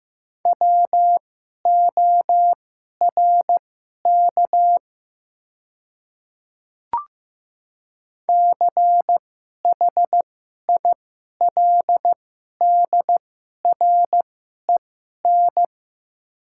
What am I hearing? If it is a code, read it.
Morse code, heard as a keyed tone, about 15 wpm, spaced as for 11 wpm, 690 Hz. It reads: WORK CHILDREN
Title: 2x Character Spacing - Top 500 Words - 15wpm